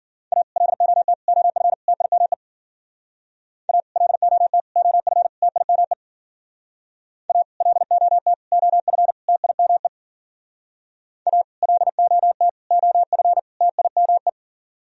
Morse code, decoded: A LOT OF TIME A LOT OF TIME A LOT OF TIME A LOT OF TIME

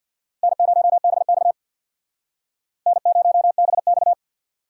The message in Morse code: N0BX N0BX

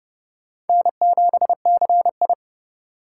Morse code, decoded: N7CI

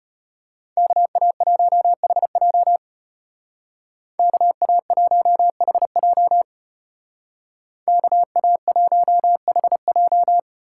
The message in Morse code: KA1HJ KA1HJ KA1HJ

